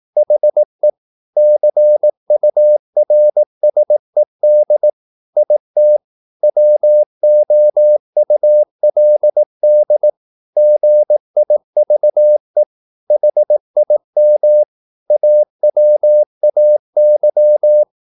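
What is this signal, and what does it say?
Morse code, heard as a keyed tone, about 18 wpm, 600 Hz. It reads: HE CURSED IT WOULD GIVE HIM AWAY
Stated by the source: Morse practice at 18 wpm, English text, from CW med Gnister.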